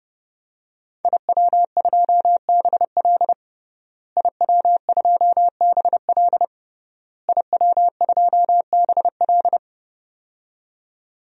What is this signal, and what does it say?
Morse code, heard as a keyed tone, about 31 wpm, 700 Hz. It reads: IW2BL IW2BL IW2BL
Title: DXCC-100 Call Signs - 5 Characters - Repeated 3x - 30wpm